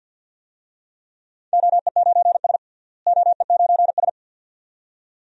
Morse code, heard as a keyed tone, about 47 wpm, 700 Hz. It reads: OE9S OE9S
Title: DXCC-100 Call Signs - 4 Characters - Repeated 2x - 50wpm